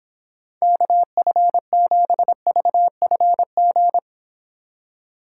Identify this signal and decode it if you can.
Morse code, heard as a keyed tone, about 26 wpm, 700 Hz. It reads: KF7VFG